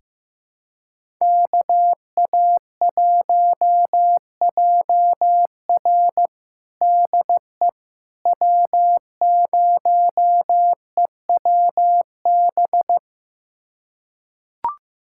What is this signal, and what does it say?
Morse code, heard as a keyed tone, about 15 wpm, 700 Hz. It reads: KA1JR DE W0EWB
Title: USA Callsign DE Callsign - 5 Characters - 15wpm